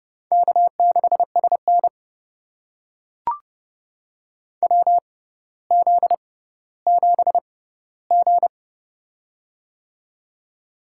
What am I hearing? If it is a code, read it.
Morse code, heard as a keyed tone, about 30 wpm, spaced as for 15 wpm, 700 Hz. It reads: K6SN WZ7G
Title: USA Call Signs - 4 Characters - 6x Character Spacing - 30wpm